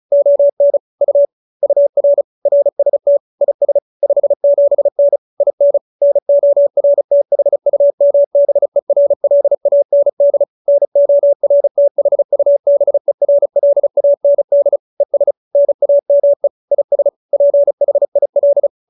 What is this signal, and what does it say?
Morse code, heard as a keyed tone, about 34 wpm, 570 Hz. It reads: ON U UR RST IS 57N IN NORTHUMBERLAND NORTHUBERLAND ES NAME IS PHIL